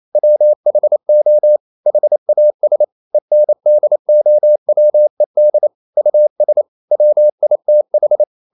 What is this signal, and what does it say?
Morse code, heard as a keyed tone, about 28 wpm, 600 Hz. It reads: WHO HAS ENDOWED US WITH
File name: CW / Practice / EnglishQuotations0128